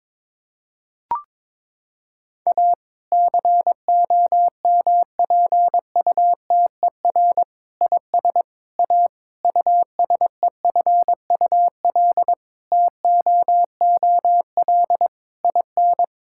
Morse code, heard as a keyed tone, about 22 wpm, 700 Hz. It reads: A COMPUTER IS A USEFUL TOOL IN
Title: Sentences from Top 1000 Words - Review Part 6 - 22wpm